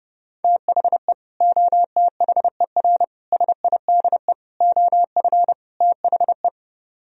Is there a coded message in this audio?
Yes: Morse code from 0.4 to 6.5 s, heard as a keyed tone, about 30 wpm, 700 Hz. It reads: THE OTHER SIDE OF THE